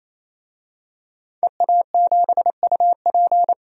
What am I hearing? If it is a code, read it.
Morse code, heard as a keyed tone, about 28 wpm, 700 Hz. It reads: EA7UP
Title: DXCC-100 Call Signs - 5 Characters - 28wpm